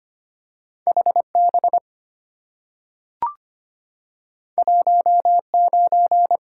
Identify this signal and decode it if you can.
Morse code, heard as a keyed tone, about 25 wpm, 700 Hz. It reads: HB 19
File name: Two Letter-Number Combinations - 25wpm